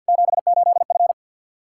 Morse code, heard as a keyed tone, about 46 wpm, 700 Hz. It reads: 78F
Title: K1USN SST QSO Practice 50wpm